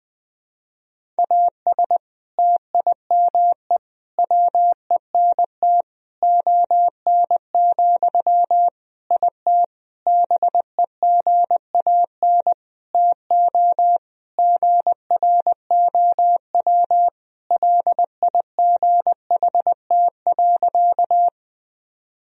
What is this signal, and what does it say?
Morse code, heard as a keyed tone, about 20 wpm, 700 Hz. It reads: AS TIME WENT ON, IT BEGAN TO GROW LIGHT.
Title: Sentences from Top 1000 Words - Review Part 1 - 20wpm